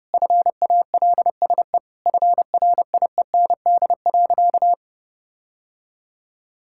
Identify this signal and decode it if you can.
Morse code, heard as a keyed tone, about 30 wpm, 700 Hz. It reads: FALSE FRIEND.
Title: Sentences from Top 1600 Words 30wpm